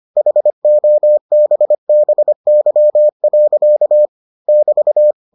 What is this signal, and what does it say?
Morse code, heard as a keyed tone, about 25 wpm, 600 Hz. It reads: HOBBY. <BT>